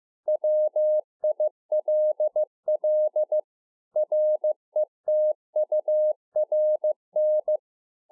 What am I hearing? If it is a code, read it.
Morse code, heard as a keyed tone, about 15 wpm, 610 Hz. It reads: WILL RETURN